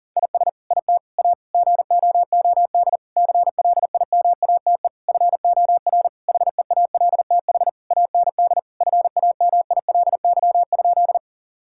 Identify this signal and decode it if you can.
Morse code, heard as a keyed tone, about 40 wpm, 700 Hz. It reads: IS IT A GOOD CLIMATE FOR HEALTH AND FAMILY?
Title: Sentences from Top 1600 Words 40wpm